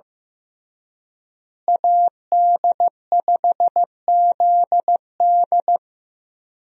Morse code, heard as a keyed tone, about 15 wpm, 700 Hz. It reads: AD5ZD
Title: USA Call Signs - 5 Characters - Repeated 3x - 15wpm